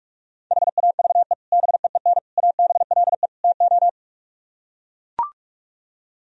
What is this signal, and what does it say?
Morse code, heard as a keyed tone, about 44 wpm, 700 Hz. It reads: HAVE BEEN ABLE TO